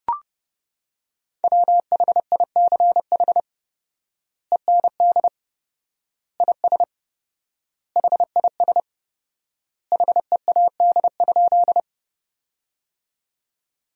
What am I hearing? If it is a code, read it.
Morse code, heard as a keyed tone, about 30 wpm, 700 Hz. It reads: WHICH END IS HIS HEAD?
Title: 4x Word-Spacing - Sentences from Top 300 Words - 30wpm